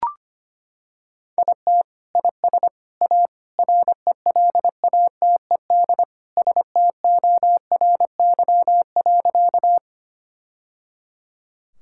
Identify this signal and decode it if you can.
Morse code, heard as a keyed tone, about 25 wpm, 700 Hz. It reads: IT IS A RELATED STORY.
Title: Sentences from Top 1000 Words - Review Part 5 - 25wpm